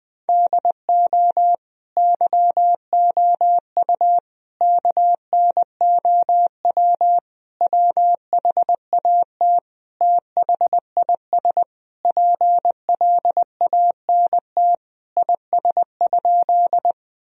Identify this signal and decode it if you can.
Morse code, heard as a keyed tone, about 21 wpm, 700 Hz. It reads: DO YOU KNOW WHAT THIS PLANT IS?
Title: Sentences from Top 1000 Words - Review Part 1 - 20wpm